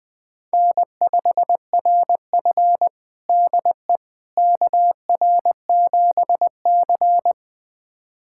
Morse code, heard as a keyed tone, about 20 wpm, 700 Hz. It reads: N5RF DE KR7C
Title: USA Callsign DE Callsign - 4 Characters - 20wpm